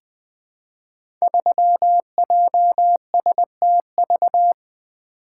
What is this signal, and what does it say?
Morse code, heard as a keyed tone, about 20 wpm, 700 Hz. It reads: 3JSTV
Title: Code Groups - 5 Letter:Numbers - 20wpm